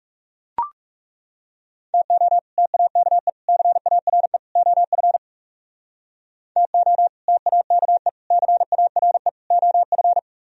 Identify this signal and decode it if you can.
Morse code, heard as a keyed tone, about 39 wpm, 700 Hz. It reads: TO TAKE CARE OF TO TAKE CARE OF